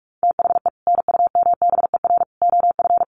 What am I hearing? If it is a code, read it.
Morse code, heard as a keyed tone, about 44 wpm, 700 Hz. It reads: THE NUMBER OF
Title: Sets of 5 Words Encore 45wpm